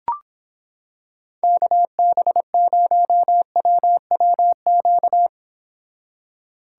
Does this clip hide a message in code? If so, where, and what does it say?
Yes: Morse code from 1.4 to 5.3 s, heard as a keyed tone, about 26 wpm, 700 Hz. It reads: KB0WWQ